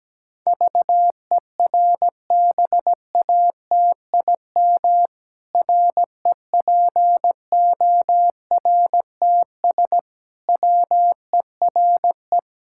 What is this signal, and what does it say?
Morse code, heard as a keyed tone, about 17 wpm, 700 Hz. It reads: VERBATIM REPORTS WERE
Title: Pangram Challenge v2 - 17wpm